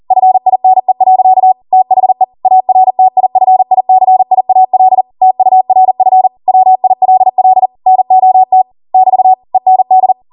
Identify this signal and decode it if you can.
Morse code, heard as a keyed tone, about 40 wpm, 750 Hz. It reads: FINE. THE ARTIFICIAL TURF WILL NOT <BT> END
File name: Morse / 40WPM